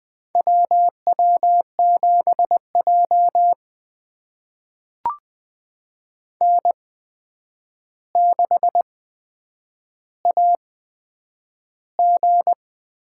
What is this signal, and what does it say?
Morse code, heard as a keyed tone, about 20 wpm, spaced as for 8 wpm, 700 Hz. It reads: W W7J N6AG
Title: USA Call Signs - 4 Characters - 8x Character Spacing - 20wpm